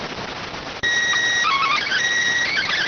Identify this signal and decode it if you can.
SSTV (slow-scan television), PD 290.